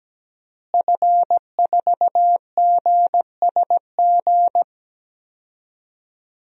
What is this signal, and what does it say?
Morse code, heard as a keyed tone, about 17 wpm, 700 Hz. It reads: F4GSG
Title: DXCC-100 Call Signs - 5 Characters - Repeated 3x - 17wpm